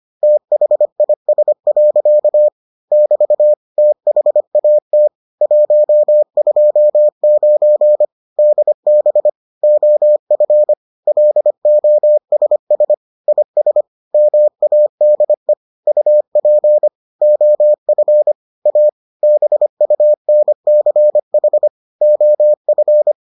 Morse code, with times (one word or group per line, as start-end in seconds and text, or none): 0.2-2.5 THIS.
2.9-3.5 <BT>
3.8-5.1 THAT
5.4-8.1 129
8.4-9.3 DB
9.6-10.7 OF
11.1-12.9 LOSS
13.3-13.8 IS
14.1-15.5 MADE
15.9-16.9 UP
17.2-18.3 OF
18.7-18.9 A
19.2-21.7 BUNCH
22.0-23.1 OF